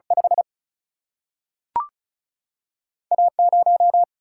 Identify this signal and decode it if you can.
Morse code, heard as a keyed tone, about 36 wpm, 700 Hz. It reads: 5 A 0
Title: Single-Letter Single-Number 35wpm